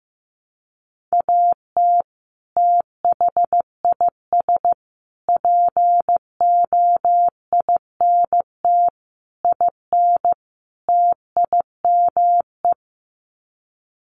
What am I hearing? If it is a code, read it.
Morse code, heard as a keyed tone, about 15 wpm, 700 Hz. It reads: AT THIS POINT IN TIME
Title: Sets of 5 Words Encore 15wpm